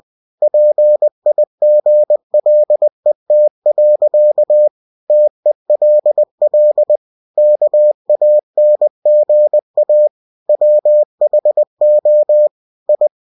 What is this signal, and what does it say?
Morse code, heard as a keyed tone, about 20 wpm, 600 Hz. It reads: PIGLET. TELL KANGA WHO I
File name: cw-books / winnie-the-pooh / winnie-20wpm-page-0073